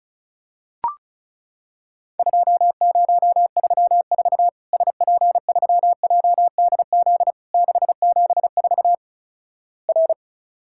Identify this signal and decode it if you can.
Morse code, heard as a keyed tone, about 34 wpm, 700 Hz. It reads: 2034 SP3JDZ 674 R
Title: Worked All Europe DX Contest - QTC Exchange - 35wpm